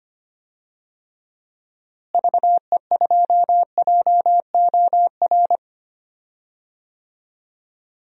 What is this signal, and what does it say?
Morse code, heard as a keyed tone, about 25 wpm, 700 Hz. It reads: VE2JOR
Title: DXCC-100 Call Signs - 6 Characters - Repeated 2x - 25wpm